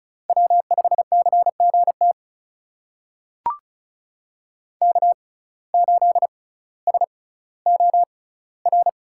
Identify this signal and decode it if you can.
Morse code, heard as a keyed tone, about 36 wpm, spaced as for 18 wpm, 700 Hz. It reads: W5CGT K8SOR
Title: USA Call Signs - 5 Characters - 6x Character Spacing - 35wpm